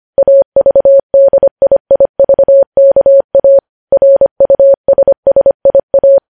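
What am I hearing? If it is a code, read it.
Morse code, heard as a keyed tone, about 25 wpm, 550 Hz. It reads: AVDIIVKA RUSSIA
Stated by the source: Morse code news headlines at 25 words per minute.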